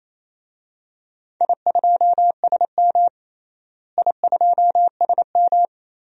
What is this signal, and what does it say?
Morse code, heard as a keyed tone, about 28 wpm, 700 Hz. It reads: I2SM I2SM
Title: DXCC-100 Call Signs - 4 Characters - Repeated 3x - 28wpm